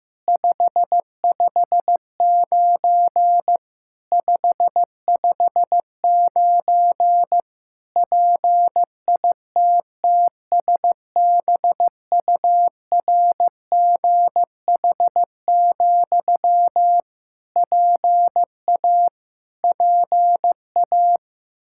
Morse code, heard as a keyed tone, about 15 wpm, 700 Hz. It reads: 559 559 PITTSBURGH, PA PA
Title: Practice QSO - Exchange 1 - Beginning - 15wpm